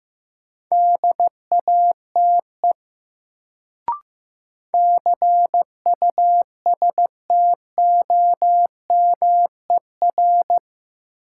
Morse code, heard as a keyed tone, about 15 wpm, 700 Hz. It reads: DATE CUSTOMER
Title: Sentences from Top 1100 Words - New Word Drill Alt - 15wpm